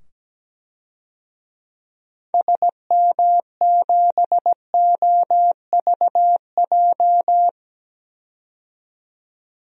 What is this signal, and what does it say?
Morse code, heard as a keyed tone, about 17 wpm, 700 Hz. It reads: SM7OVJ
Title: DXCC-100 Call Signs - 6 Characters - 17wpm